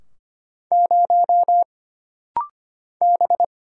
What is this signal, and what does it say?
Morse code, heard as a keyed tone, about 25 wpm, 700 Hz. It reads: 0 B
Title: Single Letter-Number Alt - Rapid-Fire 25wpm